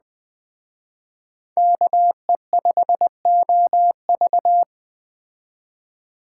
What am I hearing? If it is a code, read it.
Morse code, heard as a keyed tone, about 20 wpm, 700 Hz. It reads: KE5OV
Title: Speed Racing - USA 2x2 Call Signs 17wpm